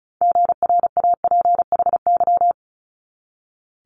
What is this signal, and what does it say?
Morse code, heard as a keyed tone, about 35 wpm, 700 Hz. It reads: GRAPHY
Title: Multi-letter Phonemes 35wpm